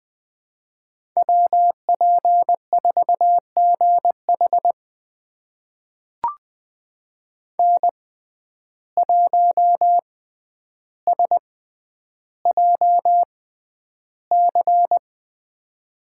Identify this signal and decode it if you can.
Morse code, heard as a keyed tone, about 20 wpm, spaced as for 10 wpm, 700 Hz. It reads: WP4GH N1SJC
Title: USA Call Signs - 5 Characters - 6x Character Spacing - 20wpm